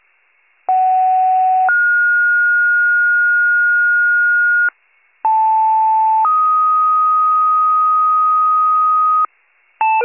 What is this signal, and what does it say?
Two-tone page, heard as a one-second tone, then a three-second tone, about 750 then 1430 Hz, about 880 then 1250 Hz, with noise. It is heard from 0.7 to 9.3 s.